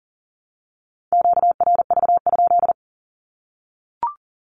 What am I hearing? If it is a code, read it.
Morse code, heard as a keyed tone, about 38 wpm, 700 Hz. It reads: QRV?
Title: Q-Codes Repeated 3x 40wpm